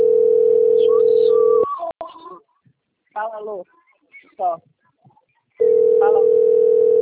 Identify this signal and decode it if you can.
Ringback tone (North America), 440 + 480 Hz, with speech.
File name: rg-1600-unknown- - - .78130